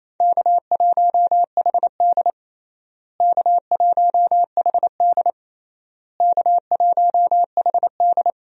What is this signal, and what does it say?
Morse code, heard as a keyed tone, about 28 wpm, 700 Hz. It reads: K1HD K1HD K1HD
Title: USA Call Signs - 4 Characters - Repeated 3x - 28wpm